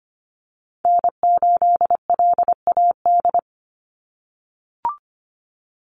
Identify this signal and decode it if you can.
Morse code, heard as a keyed tone, about 25 wpm, 700 Hz. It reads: N8LAD
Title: USA Call Signs 25wpm